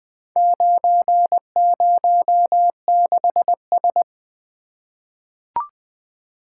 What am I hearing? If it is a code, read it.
Morse code, heard as a keyed tone, about 20 wpm, 700 Hz. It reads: 906S
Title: Four Letter-Number Combinations - 20wpm